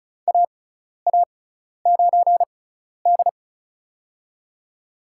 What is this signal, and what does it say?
Morse code, heard as a keyed tone, about 34 wpm, spaced as for 12 wpm, 700 Hz. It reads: AA9D
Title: USA Call Signs - 4 Characters - 6x Character Spacing - 35wpm